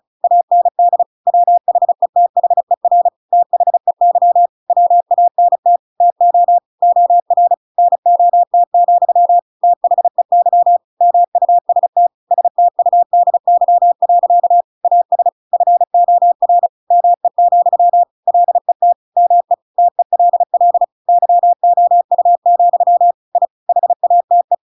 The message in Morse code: AND WHETHER THEY WANT TO OR NOT, THEY MUST STUDY. AS FOR ME, LET ME TELL YOU, I HATE

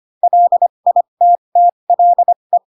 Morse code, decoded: LITTLE

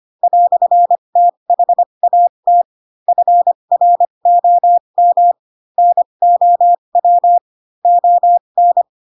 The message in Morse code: "THAT FROM NOW ON

